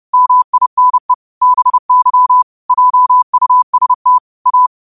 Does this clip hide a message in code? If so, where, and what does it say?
Yes: Morse code from 0.1 to 4.7 s, heard as a keyed tone, about 30 wpm, 1000 Hz. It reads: MINE BY JUST A